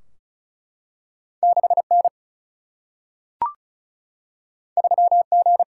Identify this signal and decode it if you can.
Morse code, heard as a keyed tone, about 36 wpm, 700 Hz. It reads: 6 N 3 G
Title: Single-Letter Single-Number 35wpm